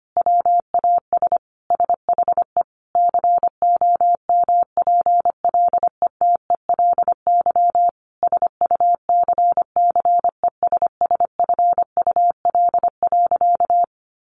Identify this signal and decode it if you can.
Morse code, heard as a keyed tone, about 25 wpm, 700 Hz. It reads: WAS SHE COMPLETELY SUCCESSFUL.
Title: Sentences from Top 1100 Words 25wpm